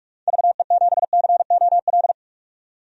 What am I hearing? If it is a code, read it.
Morse code, heard as a keyed tone, about 43 wpm, 700 Hz. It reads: VE7COL
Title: DXCC-100 Call Signs - 6 Characters - 45wpm